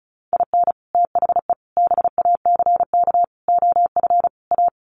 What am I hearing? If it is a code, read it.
Morse code, heard as a keyed tone, about 34 wpm, 700 Hz. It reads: IN THE BACK OF A
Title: Sets of 5 Words Encore2 35wpm